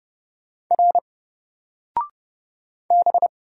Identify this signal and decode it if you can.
Morse code, heard as a keyed tone, about 30 wpm, 700 Hz. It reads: R B
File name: Single-Letter-Number-30wpm